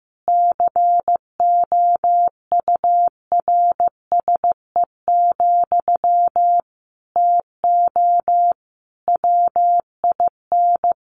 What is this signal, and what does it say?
Morse code, heard as a keyed tone, about 15 wpm, 700 Hz. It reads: COURSE, TO WIN
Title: Sentences from Top 1500 Words 15wpm